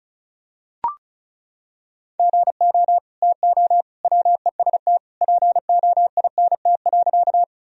Morse code, heard as a keyed tone, about 34 wpm, 700 Hz. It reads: GO TO WEST POINT.
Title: Sentences from Top 1000 Words - Review Part 2 - 35wpm